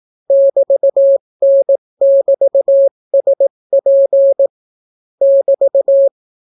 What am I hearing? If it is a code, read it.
Morse code, heard as a keyed tone, about 18 wpm, 550 Hz. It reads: <BT>N<BT>SP <BT>